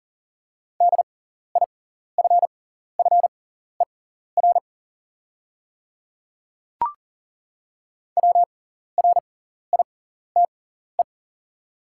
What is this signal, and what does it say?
Morse code, heard as a keyed tone, about 40 wpm, spaced as for 14 wpm, 690 Hz. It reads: DIFFER WRITE
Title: 6x Character Spacing - Top 200 Words - 40wpm